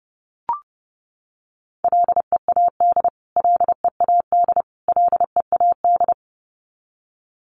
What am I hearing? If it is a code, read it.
Morse code, heard as a keyed tone, about 30 wpm, 700 Hz. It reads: LEAD LEAD LEAD
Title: Top 500 Words Repeated 30wpm